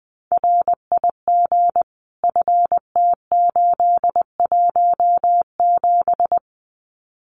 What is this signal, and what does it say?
Morse code, heard as a keyed tone, about 20 wpm, 700 Hz. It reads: RIG FT817
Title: Common QSO Phrases Encore 20wpm